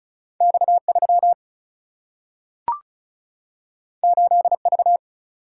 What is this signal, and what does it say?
Morse code, heard as a keyed tone, about 36 wpm, 700 Hz. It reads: X 3 8 V
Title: Single-Letter Single-Number 35wpm